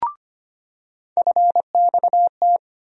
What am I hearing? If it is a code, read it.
Morse code, heard as a keyed tone, about 25 wpm, 700 Hz. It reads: FXT
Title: USA Call Sign Postfixes - 3 Letters 25wpm